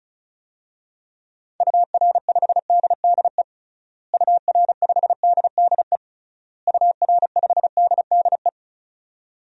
Morse code, heard as a keyed tone, about 36 wpm, 700 Hz. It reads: UR5DDE UR5DDE UR5DDE
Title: DXCC-100 Call Signs - 6 Characters - Repeated 3x - 35wpm